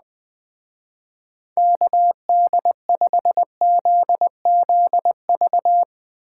Morse code, heard as a keyed tone, about 20 wpm, 700 Hz. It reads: KD5ZZV
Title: Speed Racing - USA 2x3 Call Signs 17wpm